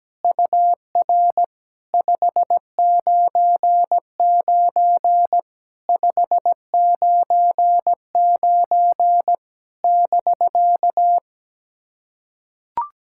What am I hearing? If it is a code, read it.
Morse code, heard as a keyed tone, about 17 wpm, 700 Hz. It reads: UR 599 599 <BK>
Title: SOTA QSO Practice 17wpm